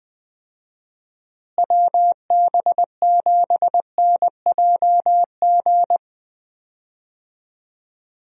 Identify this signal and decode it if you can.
Morse code, heard as a keyed tone, about 20 wpm, 700 Hz. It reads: WB7NJG
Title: USA Call Signs - 6 Characters - Repeated 3x - 20wpm